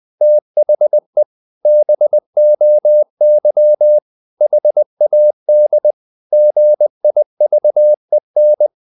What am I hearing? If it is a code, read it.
Morse code, heard as a keyed tone, about 20 wpm, 600 Hz. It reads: THE BOY HAD GIVEN
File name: the old man and the sea / the old man and the sea part 1 20wpm